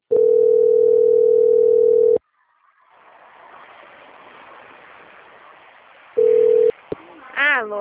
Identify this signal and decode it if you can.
Ringback tone (North America), 440 + 480 Hz.